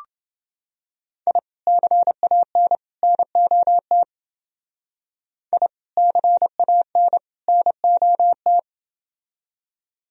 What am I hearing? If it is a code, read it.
Morse code, heard as a keyed tone, about 28 wpm, 700 Hz. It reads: I CAN NOT I CAN NOT